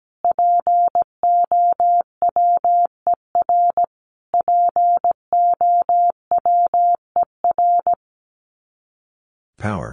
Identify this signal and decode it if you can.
Morse code, heard as a keyed tone, about 17 wpm, 700 Hz. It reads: POWER POWER
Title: Top 500 Words Repeated 17wpm